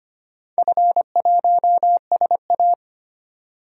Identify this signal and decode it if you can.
Morse code, heard as a keyed tone, about 25 wpm, 700 Hz. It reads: F1SA
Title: DXCC-100 Call Signs - 4 Characters - 25wpm